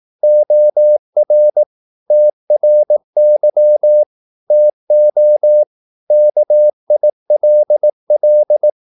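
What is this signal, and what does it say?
Morse code, heard as a keyed tone, about 18 wpm, 600 Hz. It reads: OR TRY TO KILL
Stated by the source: Morse practice at 18 wpm, English text, from CW med Gnister.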